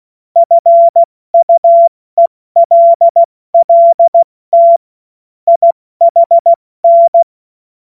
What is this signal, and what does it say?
Morse code, heard as a keyed tone, about 16 wpm, 680 Hz. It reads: FUELLT IHN
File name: Morsekurs / Witze / Witz 14. Juni 2022 70 F80